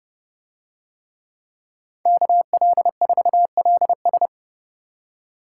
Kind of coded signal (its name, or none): Morse code